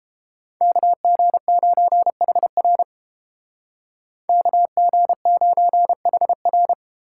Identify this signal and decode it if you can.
Morse code, heard as a keyed tone, about 29 wpm, 700 Hz. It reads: KG9HR KG9HR